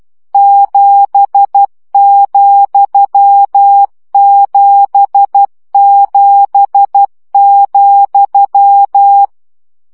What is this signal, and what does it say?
Morse code, heard as a keyed tone, about 12 wpm, 800 Hz. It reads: 7,77,
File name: CW gyakorlas z 7 vesszo 10 baud